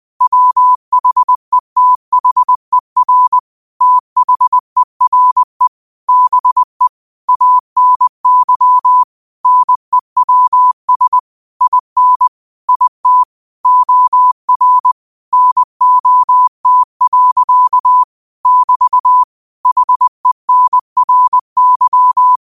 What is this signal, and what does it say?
Morse code, heard as a keyed tone, about 20 wpm, 1000 Hz. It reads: WHETHER THERE BE ANY NEWS IN IT OR NOT. <BT> HENRY
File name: qotdfiles / 20wpm